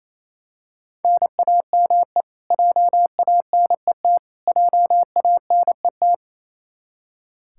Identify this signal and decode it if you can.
Morse code, heard as a keyed tone, about 28 wpm, 700 Hz. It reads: NAME JANET JANET
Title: Practice QSO - Exchange 1 - Beginning - 28wpm